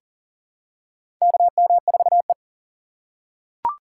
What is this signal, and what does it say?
Morse code, heard as a keyed tone, about 40 wpm, 690 Hz. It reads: KM4E
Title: USA Call Signs - 4 Characters - 2x Character Spacing - 40wpm